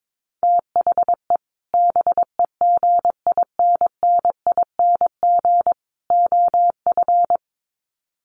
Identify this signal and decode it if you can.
Morse code, heard as a keyed tone, about 22 wpm, 700 Hz. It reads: THE BEGINNING OF
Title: Sets of 3 Words 22wpm